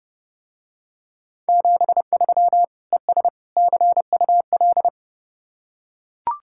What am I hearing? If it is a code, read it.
Morse code, heard as a keyed tone, about 30 wpm, 700 Hz. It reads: 73 ES CUL